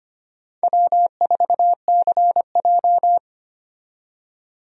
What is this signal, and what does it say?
Morse code, heard as a keyed tone, about 25 wpm, 700 Hz. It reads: W4CJ
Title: USA Call Signs - 4 Characters - 25wpm